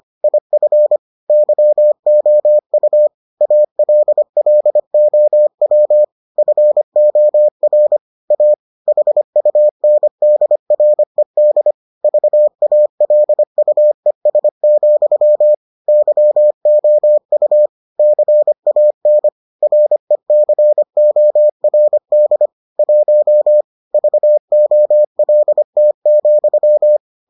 Morse code, heard as a keyed tone, about 25 wpm, 600 Hz. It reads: IF YOU ALLOW FOR A HUNDRED VALUES, YOU CAN RECORD 1 VOLT,